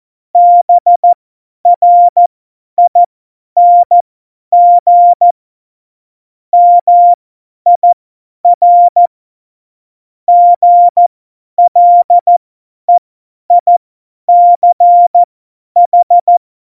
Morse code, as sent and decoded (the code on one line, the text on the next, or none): -... .-. .. -. --. / -- .. .-. / --. .-.. .. -.-. ....
BRING MIR GLEICH